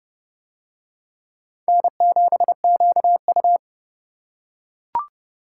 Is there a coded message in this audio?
Yes: Morse code from 1.7 to 3.6 s, heard as a keyed tone, about 28 wpm, 700 Hz. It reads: N7QU